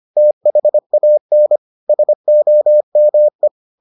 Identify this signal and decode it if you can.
Morse code, heard as a keyed tone, about 25 wpm, 600 Hz. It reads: THAN SOME